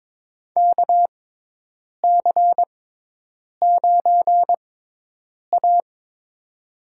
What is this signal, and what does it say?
Morse code, heard as a keyed tone, about 23 wpm, spaced as for 8 wpm, 700 Hz. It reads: KC9A
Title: USA Call Signs - 6x Character Spacing - 22wpm